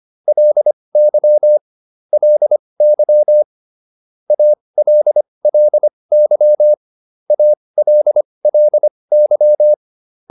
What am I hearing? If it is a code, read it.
Morse code, heard as a keyed tone, about 25 wpm, spaced as for 20 wpm, 600 Hz. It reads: LY LY ALLY ALLY